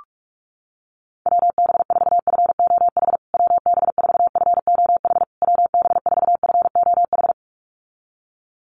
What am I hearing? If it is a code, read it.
Morse code, heard as a keyed tone, about 44 wpm, 700 Hz. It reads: WB4FOH WB4FOH WB4FOH
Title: USA Call Signs Repeated 3x 45wpm